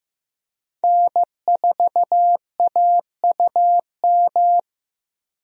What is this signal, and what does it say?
Morse code, heard as a keyed tone, about 15 wpm, 700 Hz. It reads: N4AUM
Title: USA Call Signs - 5 Characters - 15wpm